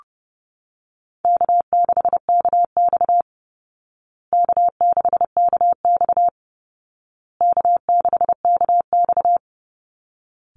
Morse code, heard as a keyed tone, about 28 wpm, 700 Hz. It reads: K6KX K6KX K6KX
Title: Easy Call Signs Repeated 3x - 30wpm